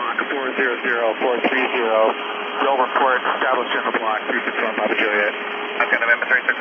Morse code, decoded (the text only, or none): none